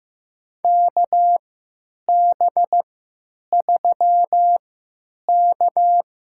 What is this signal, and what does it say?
Morse code, heard as a keyed tone, about 15 wpm, spaced as for 9 wpm, 700 Hz. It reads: KB3K
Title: USA Call Signs - 6 Characters - 3x Character Spacing - 15wpm